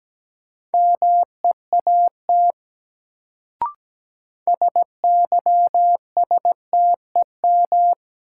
Morse code, heard as a keyed tone, about 17 wpm, 700 Hz. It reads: MEAT SYSTEM
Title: Sentences from Top 1000 Words - Review Part 3 - New Word Drill - 17wpm